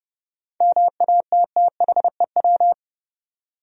Morse code, heard as a keyed tone, about 32 wpm, 700 Hz. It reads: MATTHEW
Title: USA Common Names - 30wpm